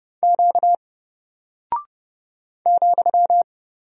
Morse code, heard as a keyed tone, about 30 wpm, 700 Hz. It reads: Q ,
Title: Single Character (Letter:Number:Punctuation:Prosign) Alt 30wpm